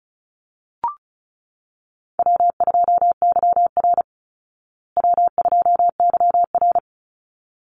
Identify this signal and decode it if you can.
Morse code, heard as a keyed tone, about 34 wpm, 700 Hz. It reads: W2YR W2YR
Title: Easy Call Signs Repeated 2x - 35wpm